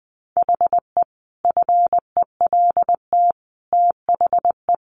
Morse code, heard as a keyed tone, about 20 wpm, 700 Hz. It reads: HE FELT THE
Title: A Sound Of Thunder 20wpm